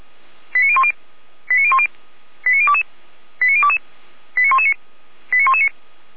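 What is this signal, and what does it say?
Five-tone selective call (ZVEI), heard as quick runs of five tones, tones of about 70 ms. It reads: 89010 89010 89022 89022 89119 89119
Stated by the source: Five-tone ZVEI selective call received on an SDR.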